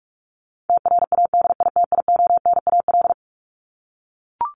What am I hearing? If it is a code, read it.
Morse code, heard as a keyed tone, about 44 wpm, 700 Hz. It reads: TRADITIONAL
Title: Top 751-1000 Words 45wpm